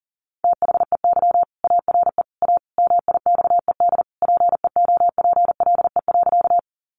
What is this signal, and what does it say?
Morse code, heard as a keyed tone, about 40 wpm, 700 Hz. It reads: THEY ARE A MIXED PEOPLE.
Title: Sentences from Top 1500 Words 40wpm